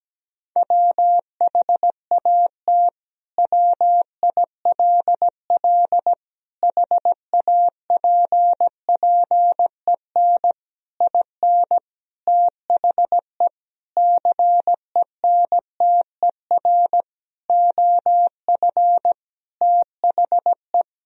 Morse code, heard as a keyed tone, about 17 wpm, 700 Hz. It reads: WHAT WILL HAPPEN IN THE CENTER OF THE
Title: Sentences from Top 1000 Words - Review Part 2 - 17wpm